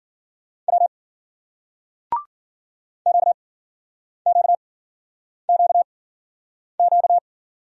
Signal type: Morse code

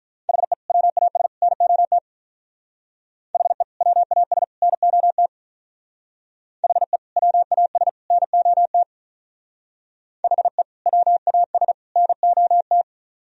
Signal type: Morse code

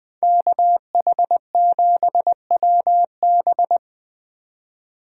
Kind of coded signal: Morse code

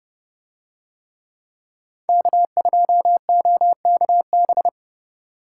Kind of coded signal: Morse code

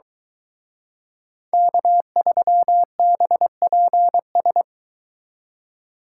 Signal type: Morse code

